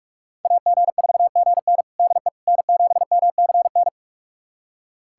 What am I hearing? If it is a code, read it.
Morse code, heard as a keyed tone, about 41 wpm, 700 Hz. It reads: AG4GN DE N7MCN
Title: USA Callsign DE Callsign - 5 Characters - 45wpm